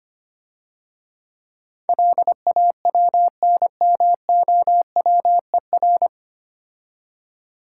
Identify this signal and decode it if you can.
Morse code, heard as a keyed tone, about 25 wpm, 700 Hz. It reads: LAWNMOWER
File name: Easy Words - Speed-Racing with Hints - 25wpm